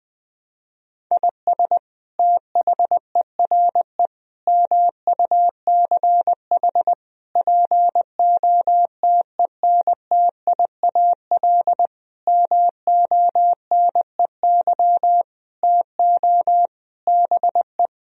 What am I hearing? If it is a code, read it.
Morse code, heard as a keyed tone, about 20 wpm, 700 Hz. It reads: IS THERE MUCH POTENTIAL MONEY TO BE
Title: Sentences from Top 1000 Words - Review Part 4 - 20wpm